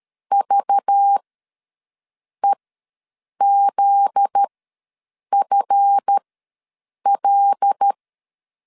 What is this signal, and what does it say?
Morse code, heard as a keyed tone, about 13 wpm, spaced as for 7 wpm, 800 Hz. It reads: VEZFL
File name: LEZIONI / 7 GF9Z / 7 GF9Z ws 12wpm 3321